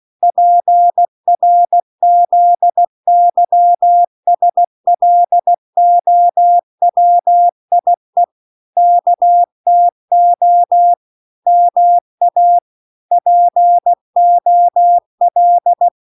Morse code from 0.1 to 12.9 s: PRZYSLOWIE KTO MA